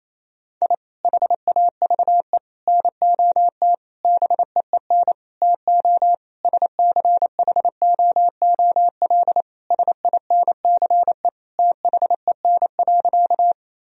Morse code, heard as a keyed tone, about 28 wpm, 700 Hz. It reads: I HAVE NOT BEEN TO SCHOOL SINCE THEN.
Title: Sentences from Top 1000 Words - Review Part 1 - 28wpm